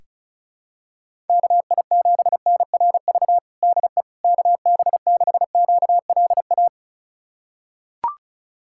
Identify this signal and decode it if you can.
Morse code, heard as a keyed tone, about 34 wpm, 700 Hz. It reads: KI7NRV DE KB6QLA
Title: USA Callsign DE Callsign 35wpm